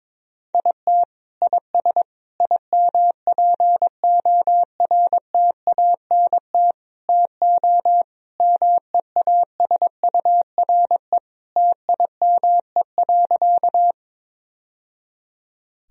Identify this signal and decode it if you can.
Morse code, heard as a keyed tone, about 22 wpm, 700 Hz. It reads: IT IS IMPORTANT TO MEASURE TIME.
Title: Sentences from Top 1000 Words - Review Part 2 - 22wpm